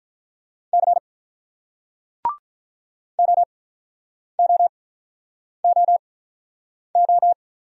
Morse code, heard as a keyed tone, about 48 wpm, 700 Hz. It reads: C O O O O